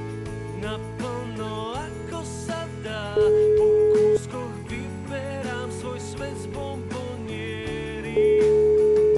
Ringback tone (Europe, 425 Hz).